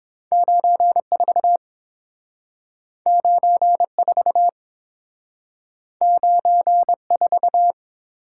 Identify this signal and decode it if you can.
Morse code, heard as a keyed tone, about 30 wpm, 700 Hz. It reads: 94 94 94